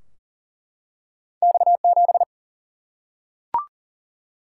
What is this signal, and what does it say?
Morse code, heard as a keyed tone, about 40 wpm, 700 Hz. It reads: X 7
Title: Single-Letter Single-Number 40wpm